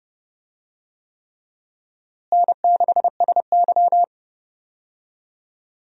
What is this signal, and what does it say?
Morse code, heard as a keyed tone, about 30 wpm, 700 Hz. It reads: N6SY
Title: USA Call Signs - 4 Characters - Repeated 3x - 30wpm